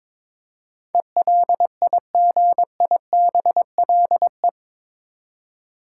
Morse code, heard as a keyed tone, about 22 wpm, 700 Hz. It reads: ELIGIBLE
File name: practice / 22wpm / Eight Letter Words - 22wpm